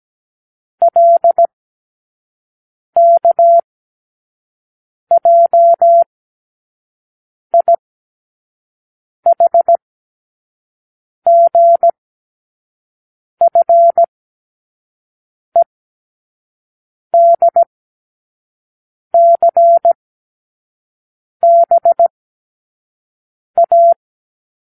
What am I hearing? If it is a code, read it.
Morse code, heard as a keyed tone, about 17 wpm, spaced as for 5 wpm, 680 Hz. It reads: LKJIHGFEDCBA
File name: k7qo / Lesson18(A-L)